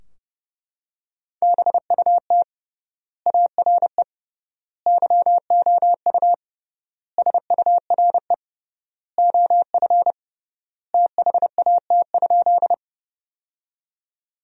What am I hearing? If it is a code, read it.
Morse code, heard as a keyed tone, about 32 wpm, 700 Hz. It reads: BUT ARE YOU SURE OF THAT?
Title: 3x Word-Spacing - Sentences from Top 300 Words - 30wpm